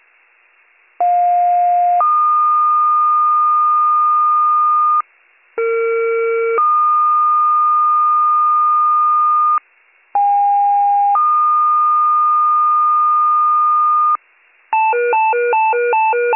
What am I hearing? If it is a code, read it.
Two-tone page, heard as a one-second tone, then a three-second tone, about 710 then 1220 Hz, about 460 then 1220 Hz, about 810 then 1220 Hz. It is heard from 1.0 to 14.2 s.